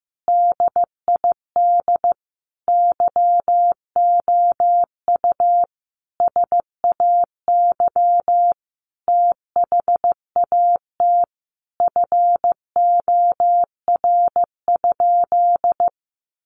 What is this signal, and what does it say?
Morse code, heard as a keyed tone, about 15 wpm, 700 Hz. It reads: DID YOU SAY THAT FOR?
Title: Sentences from Top 200 Words 15wpm